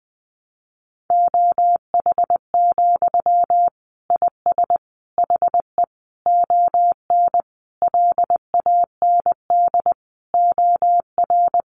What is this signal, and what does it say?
Morse code, heard as a keyed tone, about 20 wpm, 700 Hz. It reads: OH, IS HE ON LAND OR
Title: Sentences from Top 500 Words 20wpm